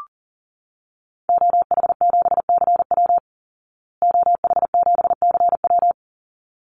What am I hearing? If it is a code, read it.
Morse code, heard as a keyed tone, about 40 wpm, 700 Hz. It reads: OH7CW OH7CW
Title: Call Signs Repeated 3x - 40wpm